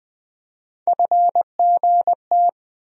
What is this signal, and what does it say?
Morse code, heard as a keyed tone, about 20 wpm, 700 Hz. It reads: FGT
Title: USA Call Sign Postfixes - 3 Letters 20wpm